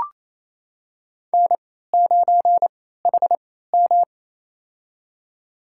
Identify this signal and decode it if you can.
Morse code, heard as a keyed tone, about 28 wpm, spaced as for 16 wpm, 700 Hz. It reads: N9HM
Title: USA Call Signs - 4 Characters - 3x Character Spacing - 28wpm